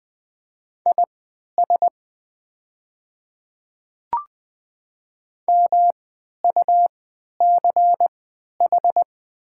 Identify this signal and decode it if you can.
Morse code, heard as a keyed tone, about 20 wpm, spaced as for 11 wpm, 700 Hz. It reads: IS MUCH
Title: 3x Character Spacing - Top 300 Words - 20wpm